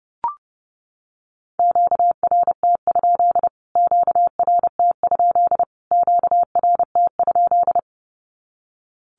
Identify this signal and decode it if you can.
Morse code, heard as a keyed tone, about 30 wpm, 700 Hz. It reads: QRT? QRT? QRT?
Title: Q-Codes Repeated 3x 30wpm